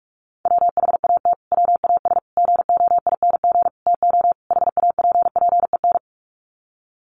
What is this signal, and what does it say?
Morse code, heard as a keyed tone, about 44 wpm, 700 Hz. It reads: WHAT WAS GOING TO HAPPEN
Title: Sets of 5 Words Encore 45wpm